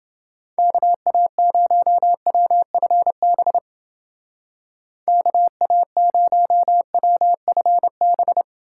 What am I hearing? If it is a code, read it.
Morse code, heard as a keyed tone, about 29 wpm, 700 Hz. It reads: KA0WFB KA0WFB